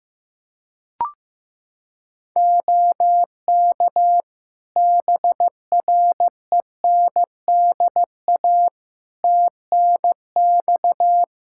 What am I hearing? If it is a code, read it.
Morse code, heard as a keyed tone, about 15 wpm, 700 Hz. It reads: OK BRENDA TNX
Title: Practice QSO - Exchange 4 - Ending - 15wpm